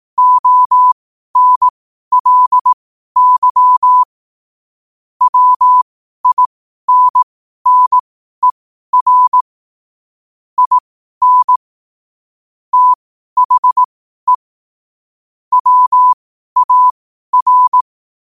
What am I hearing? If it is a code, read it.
Morse code, heard as a keyed tone, about 18 wpm, spaced as for 12 wpm, 1000 Hz. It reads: ONLY WINNER IN THE WAR
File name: qotdfiles / 12wpm